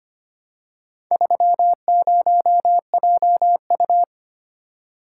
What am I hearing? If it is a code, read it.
Morse code, heard as a keyed tone, about 25 wpm, 700 Hz. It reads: 30JU
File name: practice / 25wpm / Four Letter-Number Combinations - 25wpm